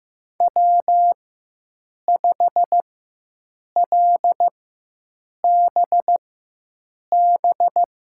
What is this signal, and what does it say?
Morse code, heard as a keyed tone, about 15 wpm, spaced as for 7 wpm, 700 Hz. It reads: W5LBB
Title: USA Call Signs - 4x Character Spacing - 15wpm